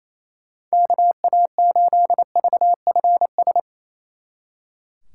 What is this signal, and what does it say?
Morse code, heard as a keyed tone, about 37 wpm, 700 Hz. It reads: KA8VFS